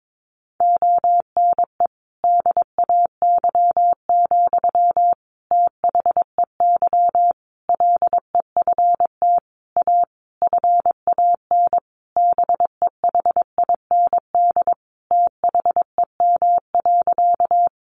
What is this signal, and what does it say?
Morse code, heard as a keyed tone, about 22 wpm, 700 Hz. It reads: ONE DAY, THEY LEFT A FAN BEHIND THEM.
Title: Sentences from Top 1300 Words 22wpm